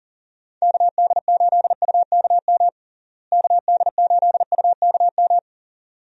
Morse code, heard as a keyed tone, about 40 wpm, 700 Hz. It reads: KD8UKM KD8UKM
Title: USA Call Signs - 6 Characters - Repeated 2x - 40wpm